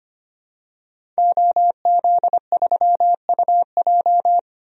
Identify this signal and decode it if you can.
Morse code, heard as a keyed tone, about 25 wpm, 700 Hz. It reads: OZ3UJ
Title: DXCC-100 Call Signs - 5 Characters - Repeated 2x - 25wpm